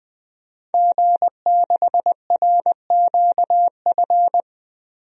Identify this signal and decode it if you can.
Morse code, heard as a keyed tone, about 21 wpm, 700 Hz. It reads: G6RQF
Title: DXCC-100 Call Signs - 5 Characters - Repeated 3x - 20wpm